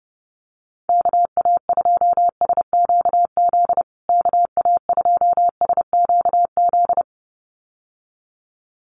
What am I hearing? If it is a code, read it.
Morse code, heard as a keyed tone, about 30 wpm, 700 Hz. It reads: KA2SQZ KA2SQZ
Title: USA Call Signs Repeated 2x 30wpm